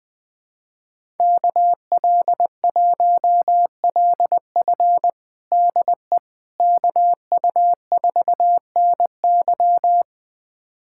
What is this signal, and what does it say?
Morse code, heard as a keyed tone, about 20 wpm, 700 Hz. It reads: KL1LF DE KU4NY
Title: USA Callsign DE Callsign - 5 Characters - 20wpm